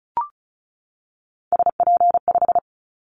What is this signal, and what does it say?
Morse code, heard as a keyed tone, about 36 wpm, 700 Hz. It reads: SP5
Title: Callsign Prefixes for Top 25 DXCC - 35wpm